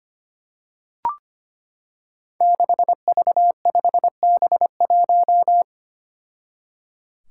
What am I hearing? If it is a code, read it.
Morse code, heard as a keyed tone, about 25 wpm, 700 Hz. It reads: 6V5B1
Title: Five Challenging Letter-Number Combinations 25wpm